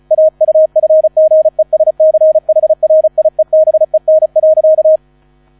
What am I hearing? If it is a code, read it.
Morse code, heard as a keyed tone, about 35 wpm, 630 Hz. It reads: AUFGESCHRIEBEN.